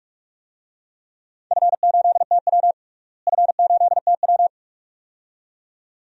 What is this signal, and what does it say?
Morse code, heard as a keyed tone, about 43 wpm, 700 Hz. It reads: F8TW F8TW
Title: DXCC-100 Call Signs - 4 Characters - Repeated 2x - 45wpm